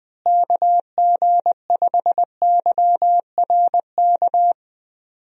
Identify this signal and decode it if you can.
Morse code, heard as a keyed tone, about 21 wpm, 700 Hz. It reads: KG5YRK
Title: USA Call Signs - 6 Characters - Repeated 3x - 20wpm